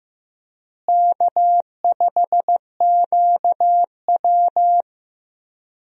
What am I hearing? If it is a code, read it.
Morse code, heard as a keyed tone, about 15 wpm, 700 Hz. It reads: K5QW
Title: USA Call Signs - 4 Characters - Repeated 3x - 15wpm